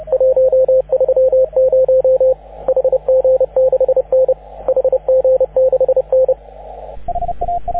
Morse code, heard as a keyed tone, about 31 wpm, 530 Hz, with radio noise. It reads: 130 HG6N HG6N
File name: AUDIO2019 / HG6N CW SSB MIX